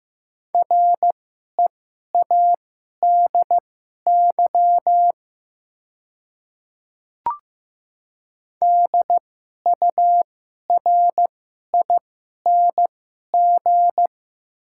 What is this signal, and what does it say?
Morse code, heard as a keyed tone, about 15 wpm, spaced as for 11 wpm, 700 Hz. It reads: READY DURING
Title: 2x Character Spacing - Top 500 Words - 15wpm